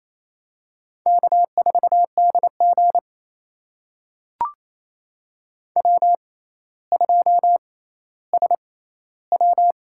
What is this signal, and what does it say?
Morse code, heard as a keyed tone, about 28 wpm, spaced as for 14 wpm, 700 Hz. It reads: K4DG W2SW
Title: USA Call Signs - 6x Character Spacing - 28wpm